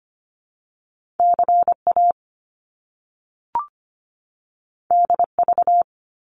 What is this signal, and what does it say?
Morse code, heard as a keyed tone, about 25 wpm, 700 Hz. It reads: CA DV
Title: Top Two Letter Combinations in Words 25wpm